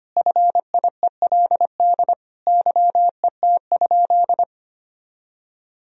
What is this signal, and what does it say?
Morse code, heard as a keyed tone, about 25 wpm, 700 Hz. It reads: FIELD YET?
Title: Sentences from Top 1000 Words - Review Part 2 - 25wpm